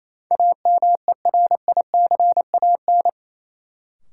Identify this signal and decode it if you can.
Morse code, heard as a keyed tone, about 28 wpm, 700 Hz. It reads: AMERICAN
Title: Sets of 2 Words - New Word Drill - 28wpm